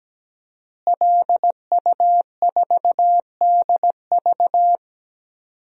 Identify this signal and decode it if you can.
Morse code, heard as a keyed tone, about 17 wpm, 700 Hz. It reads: LU4DV
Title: DXCC-100 Call Signs - 5 Characters - Repeated 2x - 17wpm